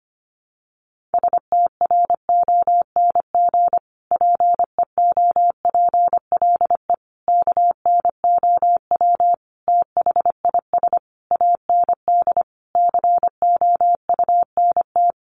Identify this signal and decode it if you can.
Morse code, heard as a keyed tone, about 25 wpm, 700 Hz. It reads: STRONG PEOPLE KNOW THIS AND COUNT
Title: Sentences from Top 1500 Words 25wpm